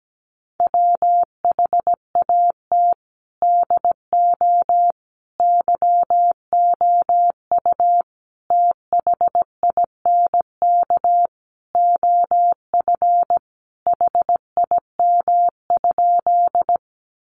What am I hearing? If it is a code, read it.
Morse code, heard as a keyed tone, about 17 wpm, 700 Hz. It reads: WHAT DO YOU THINK OF HIM?
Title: Sentences from Top 200 Words 17wpm